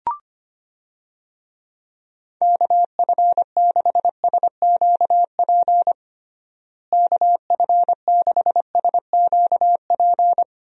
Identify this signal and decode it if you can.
Morse code, heard as a keyed tone, about 25 wpm, 700 Hz. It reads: KF6SQP KF6SQP
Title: USA Call Signs - 6 Characters - Repeated 2x - 25wpm